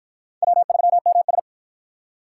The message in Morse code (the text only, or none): W3MS